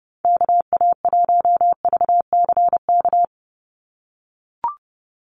Morse code, heard as a keyed tone, about 30 wpm, 700 Hz. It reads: KA1VCK
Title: USA Call Signs 30wpm